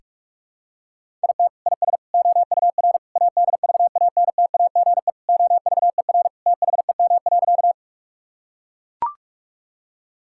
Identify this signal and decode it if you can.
Morse code, heard as a keyed tone, about 44 wpm, 700 Hz. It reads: IT IS OUR ADVANTAGE OVER THEM.
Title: Sentences from Top 1000 Words - Review Part 6 - 45wpm